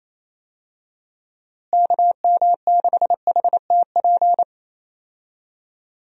Morse code, heard as a keyed tone, about 28 wpm, 700 Hz. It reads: KM6HTP
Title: USA Call Signs - 6 Characters - Repeated 2x - 28wpm